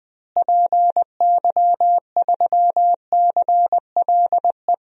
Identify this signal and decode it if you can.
Morse code, heard as a keyed tone, about 21 wpm, 700 Hz. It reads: PY3CLE
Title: DXCC-100 Call Signs - 6 Characters - Repeated 2x - 20wpm